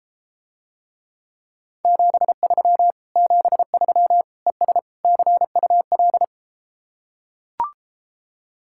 Morse code, heard as a keyed tone, about 33 wpm, 700 Hz. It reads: 73 73 ES CUL